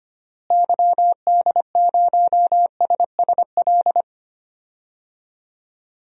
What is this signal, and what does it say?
Morse code, heard as a keyed tone, about 25 wpm, 700 Hz. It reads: YD0SSL
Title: DXCC-100 Call Signs - 6 Characters - Repeated 2x - 25wpm